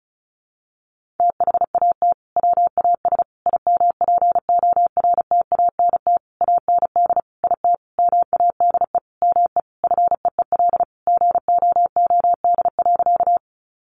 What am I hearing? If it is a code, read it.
Morse code, heard as a keyed tone, about 34 wpm, 700 Hz. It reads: THAT WAS IMPORTANT AND IT MADE ME FEEL GOOD.
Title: Sentences from Top 600 Words 35wpm